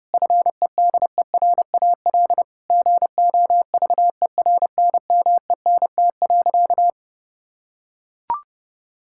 Morse code, heard as a keyed tone, about 30 wpm, 700 Hz. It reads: FEDERAL GOVERNMENT.
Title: Sentences from Top 1000 Words - Review Part 5 - 30wpm